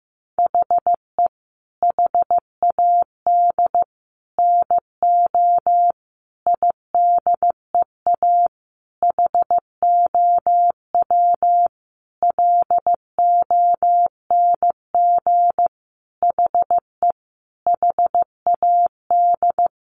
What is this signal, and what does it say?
Morse code, heard as a keyed tone, about 15 wpm, 700 Hz. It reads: HE HAD NO IDEA HOW LONG HE HAD
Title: Sentences from Top 300 Words 15wpm